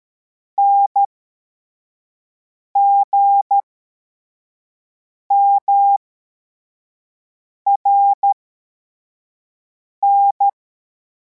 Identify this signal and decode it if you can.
Morse code, heard as a keyed tone, about 13 wpm, spaced as for 4 wpm, 800 Hz. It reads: NGMRN